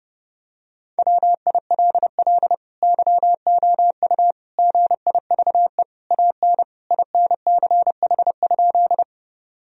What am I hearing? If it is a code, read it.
Morse code, heard as a keyed tone, about 30 wpm, 700 Hz. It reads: WILL YOU GIVE AN INCH?
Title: Sentences from Top 1000 Words - Review Part 3 - 30wpm